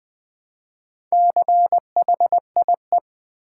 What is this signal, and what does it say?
Morse code, heard as a keyed tone, about 20 wpm, 700 Hz. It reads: CHIE
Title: Build A Word - 1 Letter At A Time - 5-Letter Words - 20wpm